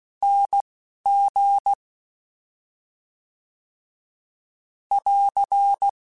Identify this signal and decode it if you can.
Morse code, heard as a keyed tone, about 16 wpm, 800 Hz. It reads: N G <AR>
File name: Special INGs-mp3s / Special INGs-mp3s / Slow ing @10